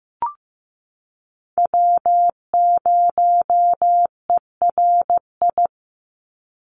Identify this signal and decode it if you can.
Morse code, heard as a keyed tone, about 15 wpm, 700 Hz. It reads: W0ERI
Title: NA SOTA Chaser Callsigns 15wpm